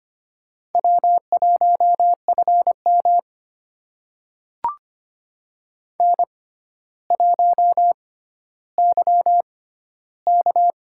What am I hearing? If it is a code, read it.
Morse code, heard as a keyed tone, about 25 wpm, spaced as for 13 wpm, 700 Hz. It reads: W1FM N1YK
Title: USA Call Signs - 4 Characters - 6x Character Spacing - 25wpm